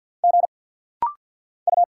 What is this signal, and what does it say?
Morse code, heard as a keyed tone, about 50 wpm, 700 Hz. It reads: G U
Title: Single Letters - Rapid-Fire 50wpm